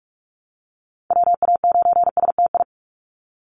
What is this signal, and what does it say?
Morse code, heard as a keyed tone, about 44 wpm, 700 Hz. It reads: WA9STI
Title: NA SOTA Chaser Callsigns 45wpm